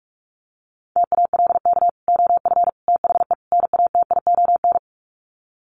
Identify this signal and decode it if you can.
Morse code, heard as a keyed tone, about 44 wpm, 700 Hz. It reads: TALK OF THE NATION
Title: Sets of 4 Words 45wpm